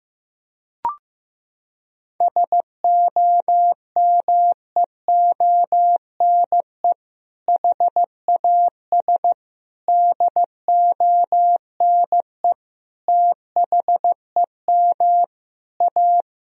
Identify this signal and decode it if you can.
Morse code, heard as a keyed tone, about 15 wpm, 700 Hz. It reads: SOMEONE HAS DONE THEM A
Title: Sentences from Top 1000 Words - Review Part 4 - 15wpm